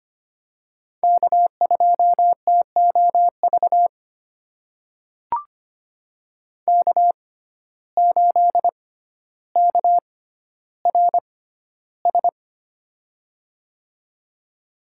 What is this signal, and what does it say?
Morse code, heard as a keyed tone, about 25 wpm, spaced as for 13 wpm, 700 Hz. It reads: K2TOV K8KRS
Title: USA Call Signs - 5 Characters - 6x Character Spacing - 25wpm